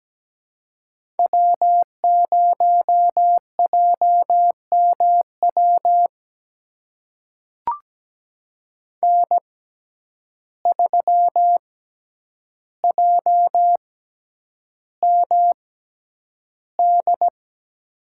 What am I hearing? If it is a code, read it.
Morse code, heard as a keyed tone, about 17 wpm, spaced as for 9 wpm, 700 Hz. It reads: W0JMW N3JMD
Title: USA Call Signs - 5 Characters - 6x Character Spacing - 17wpm